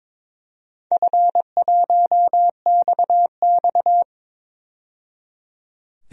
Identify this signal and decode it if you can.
Morse code, heard as a keyed tone, about 22 wpm, 700 Hz. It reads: F1XX
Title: DXCC-100 Call Signs - 4 Characters - 22wpm